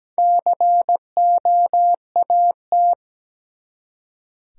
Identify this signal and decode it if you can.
Morse code, heard as a keyed tone, about 17 wpm, 700 Hz. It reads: COAT
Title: Four Letter Words - 17wpm